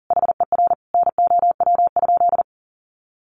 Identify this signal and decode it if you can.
Morse code, heard as a keyed tone, about 40 wpm, 700 Hz. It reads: HER NOW?
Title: Sentences from Top 900 Words 40wpm